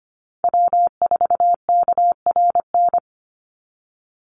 Morse code, heard as a keyed tone, about 25 wpm, 700 Hz. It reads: W4KRN
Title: NA SOTA Chaser Callsigns 25wpm